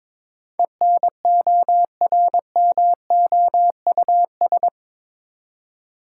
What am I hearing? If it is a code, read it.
Morse code, heard as a keyed tone, about 22 wpm, 700 Hz. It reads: ENORMOUS
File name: practice / 22wpm / Eight Letter Words - 22wpm